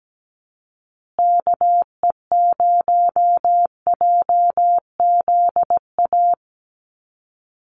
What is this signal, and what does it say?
Morse code, heard as a keyed tone, about 17 wpm, 700 Hz. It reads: KE0JZA
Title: USA Call Signs Repeated 2x 17wpm